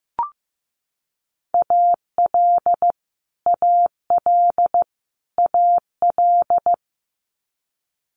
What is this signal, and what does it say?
Morse code, heard as a keyed tone, about 15 wpm, 700 Hz. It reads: AL AL AL
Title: Multi-letter Phonemes Repeated 3x 15wpm